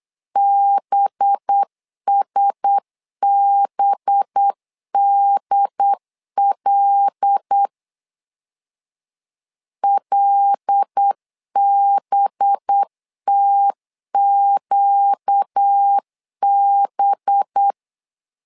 Morse code, heard as a keyed tone, about 8 wpm, 800 Hz. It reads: BSBDL LBTQB
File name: LEZIONI / 5 LBVC / 5 LB 8wpm 2816